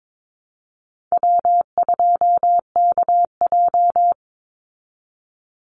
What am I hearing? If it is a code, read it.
Morse code, heard as a keyed tone, about 22 wpm, 690 Hz. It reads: W2KJ
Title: Call Signs Repeated 3x - 22wpm